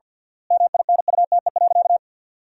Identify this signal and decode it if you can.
Morse code, heard as a keyed tone, about 53 wpm, 700 Hz. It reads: MINUTE.
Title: 3x Word-Spacing - Sentences from Top 500 Words - 50wpm